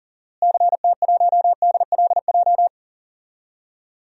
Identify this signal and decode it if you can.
Morse code, heard as a keyed tone, about 40 wpm, 700 Hz. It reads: CT1DLJ
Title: DXCC-100 Call Signs - 6 Characters - Repeated 3x - 40wpm